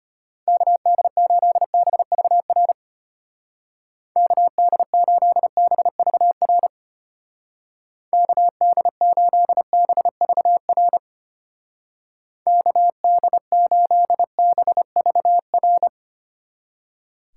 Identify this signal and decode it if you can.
Morse code, heard as a keyed tone, about 32 wpm, 700 Hz. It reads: KD8BVR KD8BVR KD8BVR KD8BVR